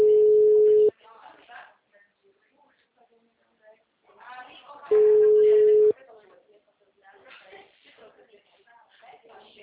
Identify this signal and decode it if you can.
Ringback tone (Europe, 425 Hz), with speech.